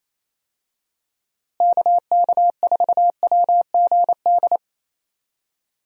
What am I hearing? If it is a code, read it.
Morse code, heard as a keyed tone, about 27 wpm, 700 Hz. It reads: KK4WGD
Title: USA Call Signs - 6 Characters - Repeated 3x - 28wpm